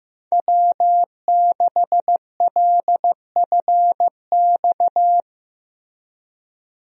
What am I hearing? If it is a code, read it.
Morse code, heard as a keyed tone, about 14 wpm, 700 Hz. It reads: W6LFX
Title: USA Call Signs - 5 Characters - Repeated 2x - 15wpm